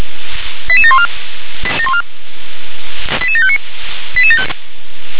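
Five-tone selective call (ZVEI), heard as quick runs of five tones, tones of about 70 ms. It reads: 88714 88759
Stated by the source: ZVEI five-tone selective call 88851 received on SDR.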